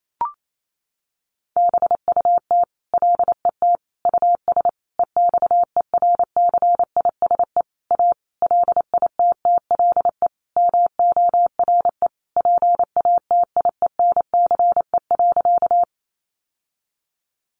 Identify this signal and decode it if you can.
Morse code, heard as a keyed tone, about 28 wpm, 700 Hz. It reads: BUT LET US EXERCISE A LITTLE MORE PATIENCE.
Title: Sentences from Top 1300 Words 28wpm